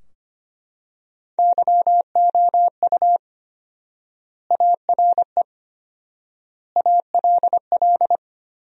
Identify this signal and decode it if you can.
Morse code, heard as a keyed tone, about 25 wpm, 700 Hz. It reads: YOU ARE ALL
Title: 4x Word-Spacing - Sentences from Top 200 Words - 25wpm